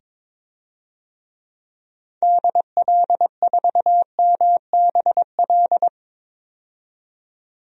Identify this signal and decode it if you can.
Morse code, heard as a keyed tone, about 21 wpm, 700 Hz. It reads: DL4MBL
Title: DXCC-100 Call Signs - 6 Characters - 22wpm